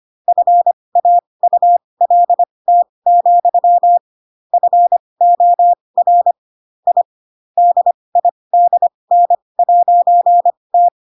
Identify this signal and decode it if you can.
Morse code, heard as a keyed tone, about 25 wpm, spaced as for 20 wpm, 700 Hz. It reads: FAULT, FOR I DIDN'T
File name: Pinocchio / Pinocchio ch7 15 25 ik1hge